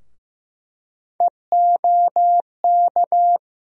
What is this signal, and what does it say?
Morse code, heard as a keyed tone, about 15 wpm, 700 Hz. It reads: EOK
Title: USA Call Sign Postfixes - 3 Letters 15wpm